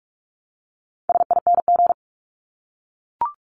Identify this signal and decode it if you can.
Morse code, heard as a keyed tone, about 44 wpm, 700 Hz. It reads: SING
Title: Top 500 Words Repeated 45wpm